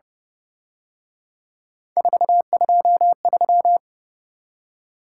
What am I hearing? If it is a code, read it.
Morse code, heard as a keyed tone, about 29 wpm, 700 Hz. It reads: 423